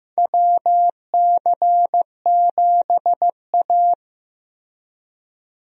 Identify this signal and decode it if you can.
Morse code, heard as a keyed tone, about 15 wpm, 700 Hz. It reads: WC7A
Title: USA Call Signs - 4 Characters - Repeated 2x - 15wpm